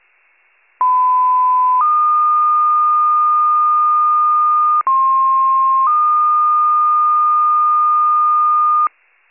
Two-tone page, heard as a one-second tone, then a three-second tone, about 1010 then 1220 Hz, about 1060 then 1220 Hz. It is heard from 0.8 to 8.9 s.